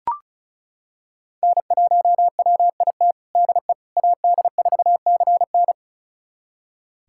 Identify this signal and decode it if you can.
Morse code, heard as a keyed tone, about 36 wpm, 690 Hz. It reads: N1WIT DE AD4CN
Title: USA Callsign DE Callsign - 5 Characters - 35wpm